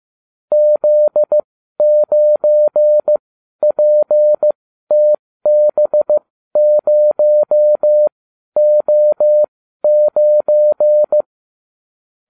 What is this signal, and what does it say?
Morse code, heard as a keyed tone, about 15 wpm, spaced as for 12 wpm, 600 Hz. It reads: Z9PTB0O9